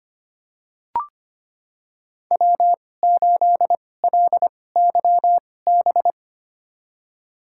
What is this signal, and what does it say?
Morse code, heard as a keyed tone, about 25 wpm, spaced as for 18 wpm, 700 Hz. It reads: W8LYB
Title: USA Call Signs - 2x Character Spacing - 25wpm